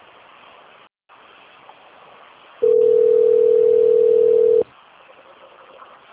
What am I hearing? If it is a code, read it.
Ringback tone (North America), 440 + 480 Hz.
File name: out- - - - - .66655